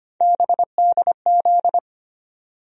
Morse code, heard as a keyed tone, about 25 wpm, 700 Hz. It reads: BDZ
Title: USA Call Sign Postfixes - 3 Letters 25wpm